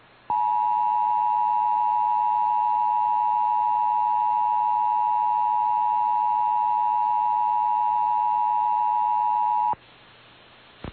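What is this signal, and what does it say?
Emergency Alert System (SAME) tones (the 853 and 960 Hz attention tone). It is heard from 0.2 to 9.8 s.
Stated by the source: EAS alert tones logged from station KBBX.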